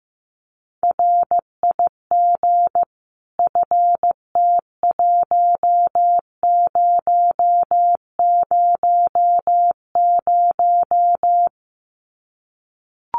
Morse code, heard as a keyed tone, about 15 wpm, 700 Hz. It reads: RIG FT1000
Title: Common QSO Phrases 15wpm